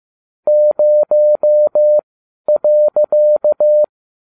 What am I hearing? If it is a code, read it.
Morse code, heard as a keyed tone, about 15 wpm, spaced as for 11 wpm, 600 Hz. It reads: 0.